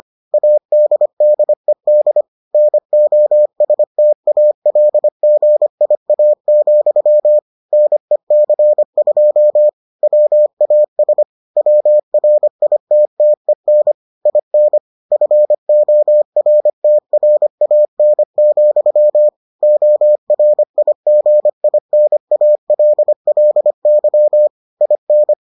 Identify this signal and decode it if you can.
Morse code, heard as a keyed tone, about 25 wpm, 600 Hz. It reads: ADDED NOSTALGIA, NEC2 WAS WRITTEN IN FORTRAN, ORIGINALLY IN